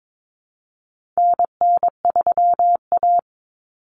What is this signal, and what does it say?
Morse code, heard as a keyed tone, about 22 wpm, 700 Hz. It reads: NN3A
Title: Common Cut Numbers - 4-Digit 22wpm